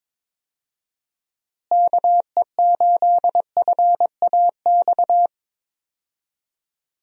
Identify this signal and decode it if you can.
Morse code, heard as a keyed tone, about 22 wpm, 700 Hz. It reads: KE8FAX
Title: USA Call Signs - 6 Characters - Repeated 3x - 22wpm